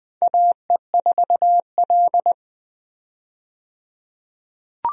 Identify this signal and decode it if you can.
Morse code, heard as a keyed tone, about 20 wpm, 700 Hz. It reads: AE4L
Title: USA Call Signs - 4 Characters - 20wpm